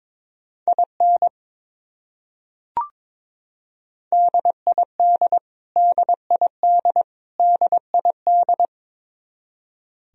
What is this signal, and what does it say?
Morse code, heard as a keyed tone, about 22 wpm, 700 Hz. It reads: IN DID DID DID
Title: Top 100 Words Repeated 22wpm